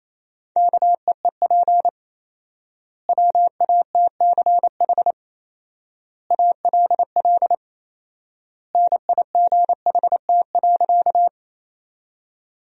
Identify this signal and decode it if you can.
Morse code, heard as a keyed tone, about 28 wpm, 700 Hz. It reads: KEEP WATCH ALL NIGHT.
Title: 4x Word-Spacing - Sentences from Top 300 Words - 28wpm